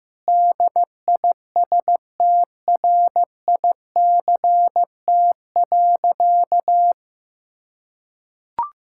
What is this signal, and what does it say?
Morse code, heard as a keyed tone, about 15 wpm, 700 Hz. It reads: DISTRICT.
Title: Sentences from Top 1600 Words 15wpm